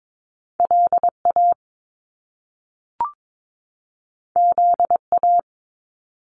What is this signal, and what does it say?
Morse code, heard as a keyed tone, about 22 wpm, 700 Hz. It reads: LA ZA
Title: Top Two Letter Combinations in Words 22wpm